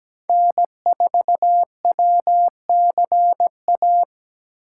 Morse code, heard as a keyed tone, about 17 wpm, 700 Hz. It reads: N4WCA
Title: USA Call Signs - 5 Characters - 4x Character Spacing - 17wpm